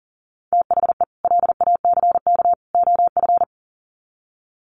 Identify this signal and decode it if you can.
Morse code, heard as a keyed tone, about 40 wpm, 700 Hz. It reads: THE LACK OF
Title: Sets of 3 Words 40wpm